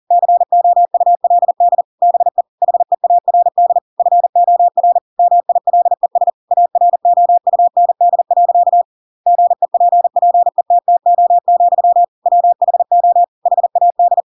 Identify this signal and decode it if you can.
Morse code, heard as a keyed tone, about 40 wpm, 700 Hz. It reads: COULD BE HEARD FOR MILES AROUND. GEPPETTO, WHO HAD
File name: Pinocchio / Pinocchio ch7 40 40 ik1hge